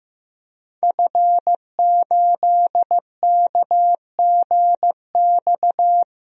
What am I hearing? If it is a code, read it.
Morse code, heard as a keyed tone, about 15 wpm, 700 Hz. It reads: F8KGX
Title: DXCC-100 Call Signs - 5 Characters - 15wpm